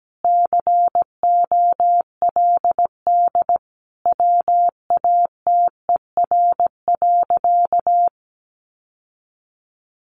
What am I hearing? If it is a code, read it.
Morse code, heard as a keyed tone, about 17 wpm, 700 Hz. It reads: COLD WATER.
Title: Sentences from Top 1200 Words 17wpm